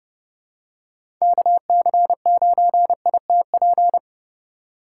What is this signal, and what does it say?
Morse code, heard as a keyed tone, about 30 wpm, 700 Hz. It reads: KC9ITP
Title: USA Call Signs - 6 Characters - 2x Character Spacing - 30wpm